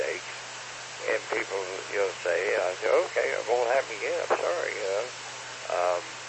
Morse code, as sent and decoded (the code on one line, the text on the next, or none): none